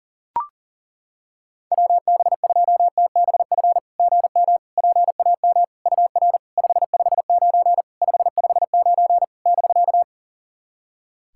Morse code, heard as a keyed tone, about 40 wpm, 690 Hz. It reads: WB2TBF GM PAM UR 559 559 <BK>
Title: SOTA QSO Practice 40wpm